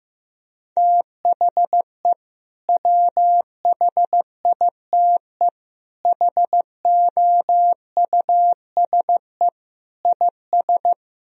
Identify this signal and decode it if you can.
Morse code, heard as a keyed tone, about 16 wpm, 700 Hz. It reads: THE WHITE HOUSE IS
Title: Sentences from Top 1000 Words - Review Part 1 - 15wpm